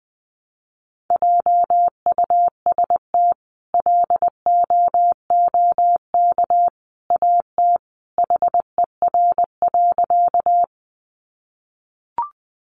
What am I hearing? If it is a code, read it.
Morse code, heard as a keyed tone, about 20 wpm, 700 Hz. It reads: JUST LOOK AT HER.
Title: Sentences from Top 200 Words 20wpm